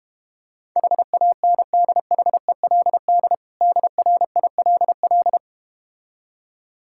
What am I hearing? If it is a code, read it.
Morse code, heard as a keyed tone, about 32 wpm, 690 Hz. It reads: HANDHELD DRILL